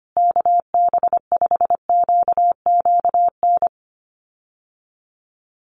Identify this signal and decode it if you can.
Morse code, heard as a keyed tone, about 25 wpm, 700 Hz. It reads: KB5QQN
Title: USA Call Signs Repeated 3x 25wpm